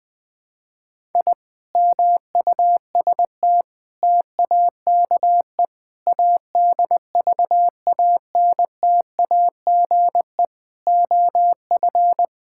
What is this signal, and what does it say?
Morse code, heard as a keyed tone, about 20 wpm, 700 Hz. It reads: I MUST TAKE ADVANTAGE OF
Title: Sentences from Top 1000 Words - Review Part 6 - 20wpm